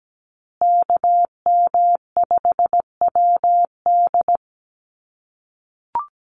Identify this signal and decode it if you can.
Morse code, heard as a keyed tone, about 17 wpm, 700 Hz. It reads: KM5WD
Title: USA Call Signs Repeated 2x 17wpm